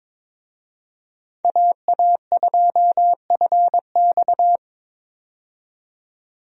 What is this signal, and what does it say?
Morse code, heard as a keyed tone, about 22 wpm, 700 Hz. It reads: AA2FX
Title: USA Call Signs - 5 Characters - 22wpm